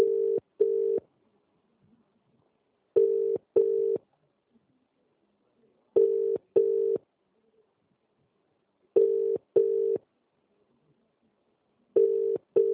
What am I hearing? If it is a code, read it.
Ringback tone (UK), 400 + 450 Hz.